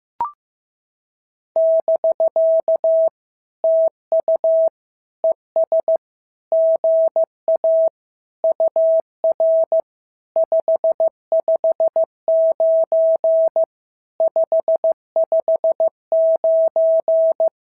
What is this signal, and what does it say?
Morse code, heard as a keyed tone, about 14 wpm, 650 Hz. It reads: <BK> TU ES GA UR 559 559
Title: SOTA QSO Practice 15wpm